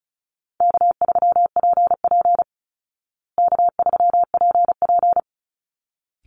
Morse code, heard as a keyed tone, about 34 wpm, 700 Hz. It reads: K3PP K3PP
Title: Easy Call Signs Repeated 2x - 35wpm